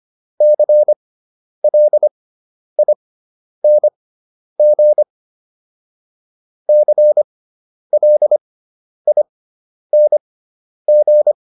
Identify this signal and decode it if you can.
Morse code, heard as a keyed tone, about 25 wpm, spaced as for 10 wpm, 600 Hz. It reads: CLING CLING
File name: ING5 10